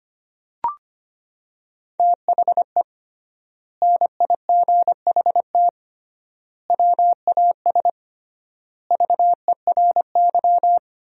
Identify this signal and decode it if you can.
Morse code, heard as a keyed tone, about 25 wpm, 700 Hz. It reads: THE NIGHT WAS VERY
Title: 3x Word-Spacing - Sentences from Top 500 Words - 25wpm